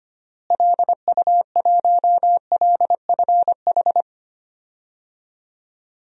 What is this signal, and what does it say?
Morse code, heard as a keyed tone, about 25 wpm, 700 Hz. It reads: LU1LFH
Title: DXCC-100 Call Signs - 6 Characters - Repeated 3x - 25wpm